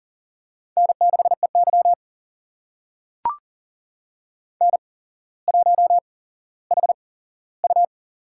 Morse code, heard as a keyed tone, about 40 wpm, spaced as for 17 wpm, 700 Hz. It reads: N6EY N1HU
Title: USA Call Signs - 4 Characters - 8x Character Spacing - 40wpm